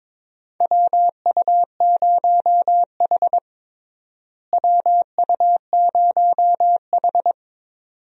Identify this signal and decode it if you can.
Morse code, heard as a keyed tone, about 22 wpm, 700 Hz. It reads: WU0H WU0H
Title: USA Call Signs - 4 Characters - Repeated 2x - 22wpm